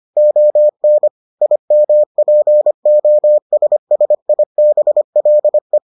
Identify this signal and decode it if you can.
Morse code, heard as a keyed tone, about 25 wpm, 600 Hz. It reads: ON IMPOSSIBLE